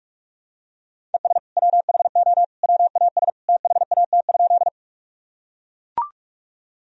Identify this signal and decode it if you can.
Morse code, heard as a keyed tone, about 44 wpm, 700 Hz. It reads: ES WHO WAS THAT?
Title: Sentences from Top 100 Words and Abbreviations 45wpm